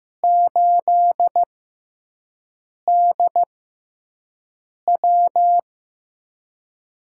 Morse code, heard as a keyed tone, about 15 wpm, spaced as for 5 wpm, 700 Hz. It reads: 8DW
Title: USA Call Signs - 5 Characters - 6x Character Spacing - 15wpm